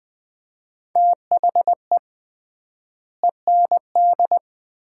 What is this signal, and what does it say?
Morse code, heard as a keyed tone, about 20 wpm, 700 Hz. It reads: THE END
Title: 3x Word-Spacing - Sets of 5 Words - 20wpm